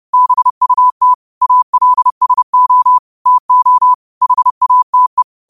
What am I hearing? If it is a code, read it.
Morse code, heard as a keyed tone, about 30 wpm, 1000 Hz. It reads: BUT ALSO TO HATE